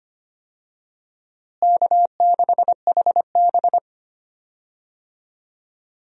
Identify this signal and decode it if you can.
Morse code, heard as a keyed tone, about 25 wpm, 700 Hz. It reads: K6HB
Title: USA Call Signs - 4 Characters - Repeated 2x - 25wpm